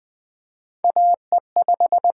Morse code, heard as a keyed tone, about 20 wpm, 700 Hz. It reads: AE5
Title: USA Call Signs - 5 Characters - 8x Character Spacing - 20wpm